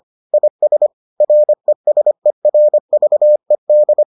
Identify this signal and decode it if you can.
Morse code, heard as a keyed tone, about 25 wpm, 600 Hz. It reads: IS RESERVED